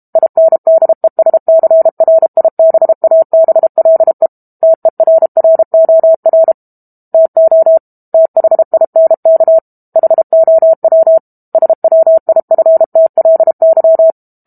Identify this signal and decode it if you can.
Morse code, heard as a keyed tone, about 32 wpm, 650 Hz. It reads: INDESCRIBABLE TERROR TO THINK HOW SWIFTLY
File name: Book Two / 06-Chapter 6 - 32 WPM